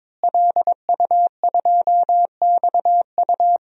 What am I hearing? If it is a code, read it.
Morse code, heard as a keyed tone, about 22 wpm, 700 Hz. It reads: LU2XU
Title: DXCC-100 Call Signs - 5 Characters - Repeated 2x - 22wpm